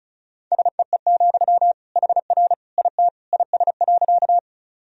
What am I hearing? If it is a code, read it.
Morse code, heard as a keyed tone, about 36 wpm, 700 Hz. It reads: SEE, HR IT IS.
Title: Sentences from Top 200 Words and Abbreviations 35wpm